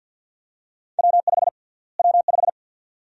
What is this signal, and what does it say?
Morse code, heard as a keyed tone, about 52 wpm, 700 Hz. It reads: W5 W5
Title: USA Call Sign Prefixes - Repeated 2x - 50wpm